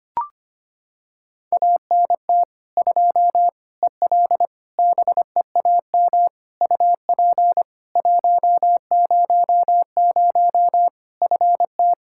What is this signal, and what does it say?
Morse code, heard as a keyed tone, about 25 wpm, 700 Hz. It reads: ANT 2 EL BEAM UP 100 FT
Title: Practice QSO - Exchange 2 - Info - 25wpm